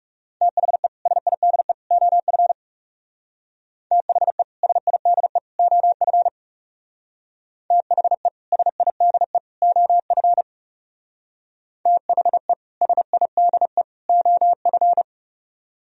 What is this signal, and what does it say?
Morse code, heard as a keyed tone, about 40 wpm, 700 Hz. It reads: THE SIDE OF THE SIDE OF THE SIDE OF THE SIDE OF